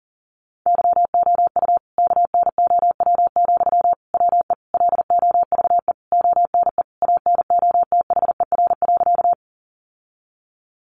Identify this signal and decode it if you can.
Morse code, heard as a keyed tone, about 40 wpm, 700 Hz. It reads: YOU KNOW, WE LOVE ONE ANOTHER.
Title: Sentences from Top 600 Words 40wpm